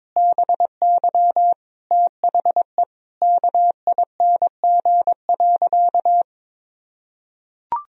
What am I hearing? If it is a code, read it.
Morse code, heard as a keyed tone, about 22 wpm, 700 Hz. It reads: BY THE KING.
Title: Sentences from Top 1000 Words - Review Part 4 - 22wpm